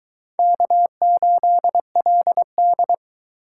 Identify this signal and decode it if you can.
Morse code, heard as a keyed tone, about 22 wpm, 700 Hz. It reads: K8LD